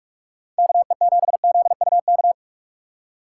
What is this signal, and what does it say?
Morse code, heard as a keyed tone, about 42 wpm, 700 Hz. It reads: KE7ZUK
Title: USA Call Signs - 3x Character Spacing - 45wpm